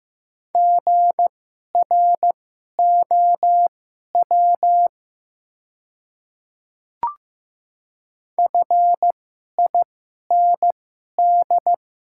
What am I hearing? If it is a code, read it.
Morse code, heard as a keyed tone, about 15 wpm, spaced as for 11 wpm, 700 Hz. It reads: GROW FIND
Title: 2x Character Spacing - Top 500 Words - 15wpm